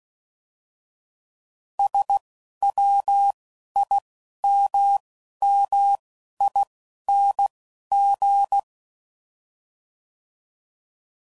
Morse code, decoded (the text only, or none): S W I M M I N G